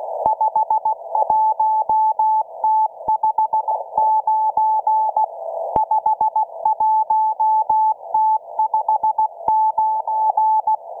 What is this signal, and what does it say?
Morse code, heard as a keyed tone, about 16 wpm, 870 Hz. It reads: 51T59 51T59